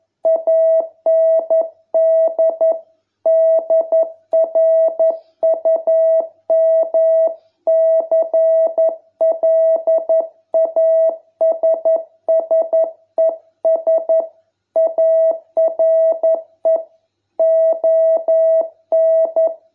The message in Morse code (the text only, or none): AND DRUM CLASSES ARE ON